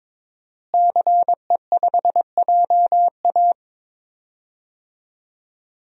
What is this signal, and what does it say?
Morse code, heard as a keyed tone, about 22 wpm, 700 Hz. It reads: CE5JA
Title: DXCC-100 Call Signs - 5 Characters - Repeated 3x - 22wpm